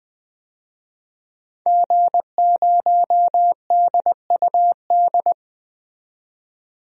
Morse code, heard as a keyed tone, about 20 wpm, 700 Hz. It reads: G0DUD
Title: DXCC-100 Call Signs - 5 Characters - 20wpm